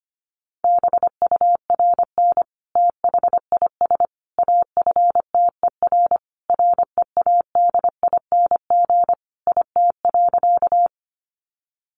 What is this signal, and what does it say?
Morse code, heard as a keyed tone, about 25 wpm, 700 Hz. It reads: BURN THIS AFTER READING IT.
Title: Sentences from Top 1300 Words 25wpm